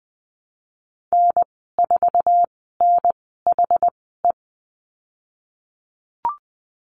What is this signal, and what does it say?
Morse code, heard as a keyed tone, about 20 wpm, spaced as for 15 wpm, 700 Hz. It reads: N4NHE
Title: 069 Lesson 12g - Letter d - Call Signs 20wpm